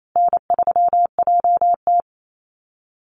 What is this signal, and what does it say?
Morse code, heard as a keyed tone, about 28 wpm, 700 Hz. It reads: N3JT
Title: Call Signs Repeated 2x - 28wpm